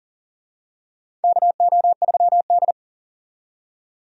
Morse code, heard as a keyed tone, about 40 wpm, 700 Hz. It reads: KO3D
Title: USA Call Signs - 4 Characters - 40wpm